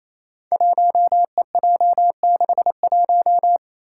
Morse code, heard as a keyed tone, about 28 wpm, 700 Hz. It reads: 1EJ61
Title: Five Challenging Letter-Number Combinations 28wpm